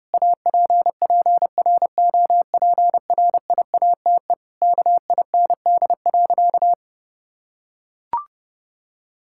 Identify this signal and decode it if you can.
Morse code, heard as a keyed tone, about 30 wpm, 700 Hz. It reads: APPROPRIATE KIND.
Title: Sentences from Top 1000 Words - Review Part 5 - 30wpm